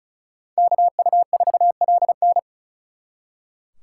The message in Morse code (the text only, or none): KU4LN